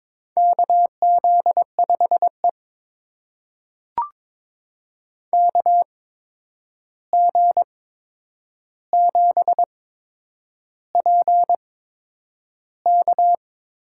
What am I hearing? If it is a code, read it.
Morse code, heard as a keyed tone, about 22 wpm, spaced as for 9 wpm, 700 Hz. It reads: KZ5E KG7PK
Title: USA Call Signs - 8x Character Spacing - 22wpm